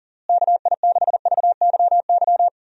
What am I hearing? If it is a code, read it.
Morse code, heard as a keyed tone, about 37 wpm, 700 Hz. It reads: KI6VYY
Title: USA Call Signs - 6 Characters - Repeated 3x - 40wpm